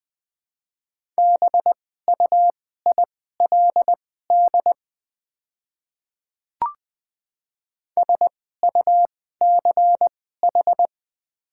Morse code, heard as a keyed tone, about 20 wpm, spaced as for 14 wpm, 700 Hz. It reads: BUILD SUCH
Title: 2x Character Spacing - Top 500 Words - 20wpm